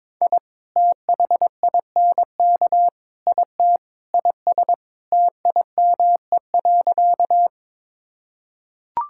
Morse code, heard as a keyed tone, about 22 wpm, 700 Hz. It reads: I THINK IT IS TIME.
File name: practice / 22wpm / Sentences with 4-Word Phrases - 22wpm